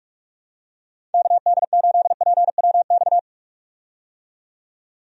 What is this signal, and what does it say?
Morse code, heard as a keyed tone, about 46 wpm, 700 Hz. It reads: KD8PWX
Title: USA Call Signs - 6 Characters - 45wpm